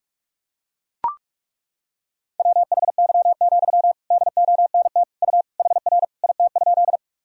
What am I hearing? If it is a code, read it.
Morse code, heard as a keyed tone, about 42 wpm, 700 Hz. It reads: WHY, DONT U HR IT?
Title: Sentences from Top 300 Words and Abbreviations 45wpm